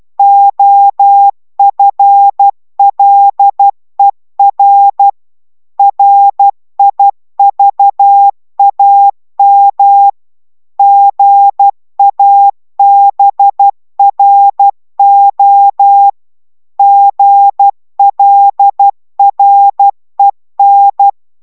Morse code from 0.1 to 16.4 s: OFLER RIVAM GABRO